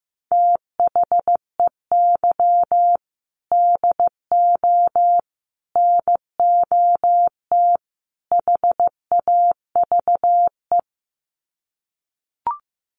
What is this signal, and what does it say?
Morse code, heard as a keyed tone, about 15 wpm, 700 Hz. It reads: THEY DO NOT HAVE
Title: Sets of 4 Words 15wpm